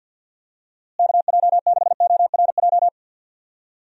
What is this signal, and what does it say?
Morse code, heard as a keyed tone, about 51 wpm, 700 Hz. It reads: KJ6ORJ
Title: USA Call Signs - 6 Characters - Repeated 2x - 50wpm